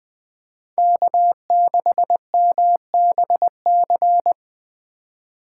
Morse code, heard as a keyed tone, about 21 wpm, 700 Hz. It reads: K6MBC
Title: Speed Racing - USA 1x3 Call Signs 17wpm